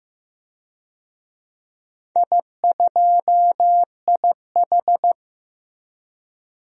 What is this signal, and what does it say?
Morse code, heard as a keyed tone, about 15 wpm, 700 Hz. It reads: I2IH
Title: DXCC-100 Call Signs - 4 Characters - Repeated 2x - 15wpm